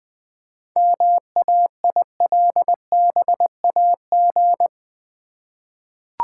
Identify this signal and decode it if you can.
Morse code, heard as a keyed tone, about 20 wpm, 700 Hz. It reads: MAILBAG
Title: Easy Words with Hints - 20wpm